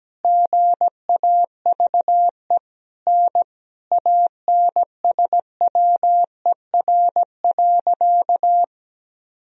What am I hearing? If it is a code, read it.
Morse code, heard as a keyed tone, about 17 wpm, 700 Hz. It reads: GAVE N ANSWER.
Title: Sentences from Top 500 Words and Abbreviations 17wpm